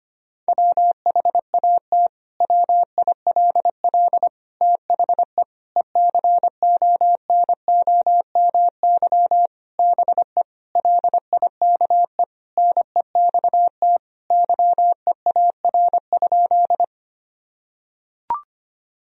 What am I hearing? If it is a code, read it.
Morse code, heard as a keyed tone, about 25 wpm, 700 Hz. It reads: WHAT WILL THE ECONOMY BE LIKE NEXT YEAR?
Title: Sentences from Top 1000 Words - Review Part 3 - 25wpm